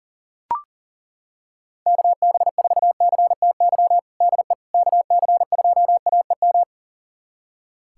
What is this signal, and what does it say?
Morse code, heard as a keyed tone, about 37 wpm, 700 Hz. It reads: KB4CTY DE KC2AEM
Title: USA Callsign DE Callsign 40wpm